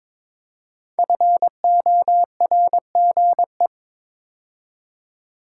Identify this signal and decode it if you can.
Morse code, heard as a keyed tone, about 22 wpm, 700 Hz. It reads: FORGE
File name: 5 Letter Words - First 3 Letters Given - 22wpm